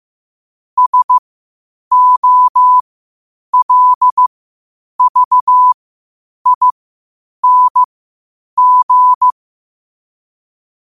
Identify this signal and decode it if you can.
Morse code, heard as a keyed tone, about 15 wpm, spaced as for 8 wpm, 1000 Hz. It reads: SOLVING
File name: qotdfiles / 8wpm